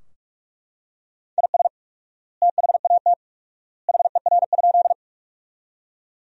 Morse code, heard as a keyed tone, about 41 wpm, 700 Hz. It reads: IS THAT HER?
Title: 4x Word-Spacing - Sentences from Top 100 Words - 45wpm